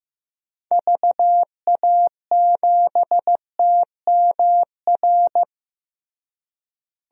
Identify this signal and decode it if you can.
Morse code, heard as a keyed tone, about 15 wpm, 700 Hz. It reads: VA7TMR
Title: DXCC-100 Call Signs - 6 Characters - Repeated 3x - 15wpm